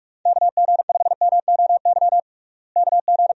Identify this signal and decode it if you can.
Morse code, heard as a keyed tone, about 46 wpm, 700 Hz. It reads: KG5MOY KG
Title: USA Call Signs - 6 Characters - Repeated 2x - 45wpm